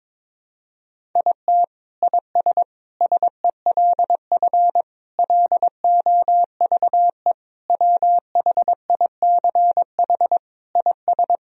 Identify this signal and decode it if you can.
Morse code, heard as a keyed tone, about 22 wpm, 700 Hz. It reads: IT IS SELF LOVE WHICH IS
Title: Sentences from Top 1000 Words - Review Part 1 - 22wpm